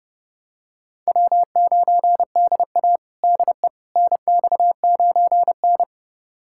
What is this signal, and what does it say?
Morse code, heard as a keyed tone, about 32 wpm, 700 Hz. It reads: W9DA DE NX9N
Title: USA Callsign DE Callsign - 4 Characters - 30wpm